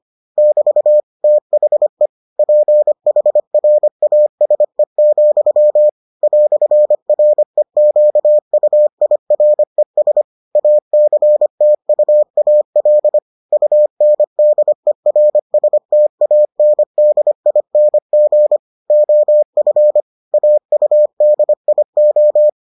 Morse code, heard as a keyed tone, about 25 wpm, 600 Hz. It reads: <BT> THE PHRASE, "REQUIRES ACTUAL UNDERSTANDING OF AUDIO